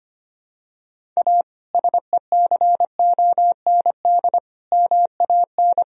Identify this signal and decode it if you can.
Morse code, heard as a keyed tone, about 25 wpm, 700 Hz. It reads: A SECOND MAN
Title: Sentences from Top 1000 Words - Review Part 1 - 25wpm